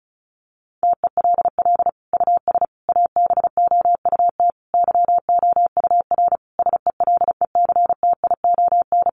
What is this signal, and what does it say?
Morse code, heard as a keyed tone, about 34 wpm, 700 Hz. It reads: TELL US ABOUT YOUR SELECTION
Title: Sentences from Top 1400 Words 35wpm